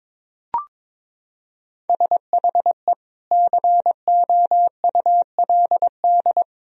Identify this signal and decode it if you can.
Morse code, heard as a keyed tone, about 22 wpm, 700 Hz. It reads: SHE COULD
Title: Sentences from Top 1000 Words - Review Part 3 - 22wpm